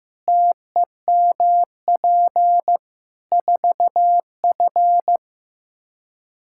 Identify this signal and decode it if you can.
Morse code, heard as a keyed tone, about 16 wpm, 700 Hz. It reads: TEMP 4F
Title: Practice QSO - Exchange 2 - Info - 15wpm